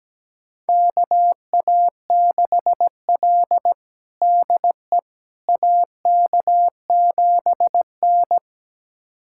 Morse code, heard as a keyed tone, about 17 wpm, 700 Hz. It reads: KA6L DE AK7N
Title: USA Callsign DE Callsign - 4 Characters - 17wpm